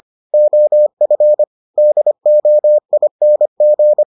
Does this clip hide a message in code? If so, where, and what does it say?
Yes: Morse code from 0.3 to 4.0 s, heard as a keyed tone, about 25 wpm, 600 Hz. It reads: OF DOING